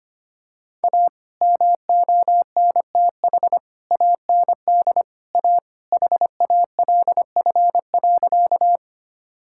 Morse code, heard as a keyed tone, about 25 wpm, 700 Hz. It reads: A MONTH AND A HALF.
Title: Sentences from Top 1000 Words - Review Part 5 - 25wpm